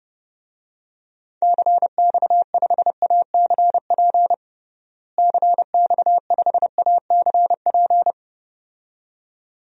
Morse code, heard as a keyed tone, about 30 wpm, 700 Hz. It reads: CX5ACP CX5ACP
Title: DXCC-100 Call Signs - 6 Characters - Repeated 2x - 30wpm